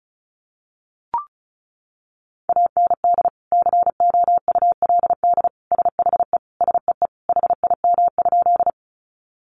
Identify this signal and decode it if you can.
Morse code, heard as a keyed tone, about 34 wpm, 700 Hz. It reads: AND COULD SHE SEE HIM?
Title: Sentences from Top 100 Words 35wpm